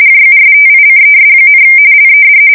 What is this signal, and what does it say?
RTTY, about 45.45 baud. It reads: JUSTICES, LED
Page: 170 Shift 2125/2295